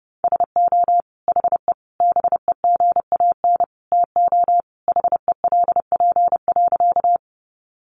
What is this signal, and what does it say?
Morse code, heard as a keyed tone, about 30 wpm, 700 Hz. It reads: SO HE BEGAN TO HELP.
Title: Sentences from Top 300 Words 30wpm